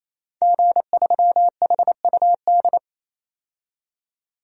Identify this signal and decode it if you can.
Morse code, heard as a keyed tone, about 27 wpm, 700 Hz. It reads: G3HUD
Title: DXCC-100 Call Signs - 5 Characters - 28wpm